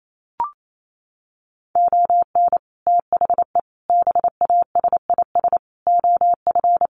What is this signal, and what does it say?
Morse code, heard as a keyed tone, about 28 wpm, 700 Hz. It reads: ON THE BASIS OF
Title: Sets of 5 Words 28wpm